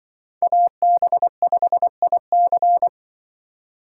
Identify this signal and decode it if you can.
Morse code, heard as a keyed tone, about 24 wpm, 700 Hz. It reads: AB5IC
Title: Speed Racing - USA 2x2 Call Signs 20wpm